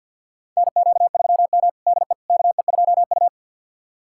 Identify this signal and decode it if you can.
Morse code, heard as a keyed tone, about 53 wpm, 700 Hz. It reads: NQ3M DE KE2U
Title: USA Callsign DE Callsign - 4 Characters - 50wpm